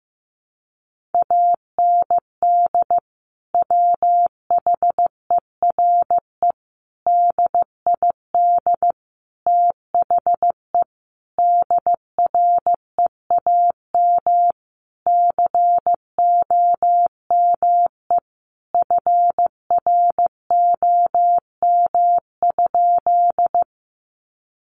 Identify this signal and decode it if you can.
Morse code, heard as a keyed tone, about 15 wpm, 700 Hz. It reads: AND WHERE DID THE DREAM COME FROM?
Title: Sentences from Top 1500 Words 15wpm